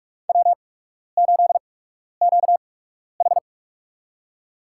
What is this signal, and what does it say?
Morse code, heard as a keyed tone, about 46 wpm, spaced as for 12 wpm, 700 Hz. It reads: W8QH
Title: USA Call Signs - 4 Characters - 8x Character Spacing - 45wpm